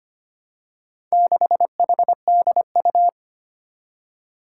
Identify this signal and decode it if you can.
Morse code, heard as a keyed tone, about 25 wpm, 700 Hz. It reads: 6HDU
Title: Four Challenging Letter-Number Combinations 25wpm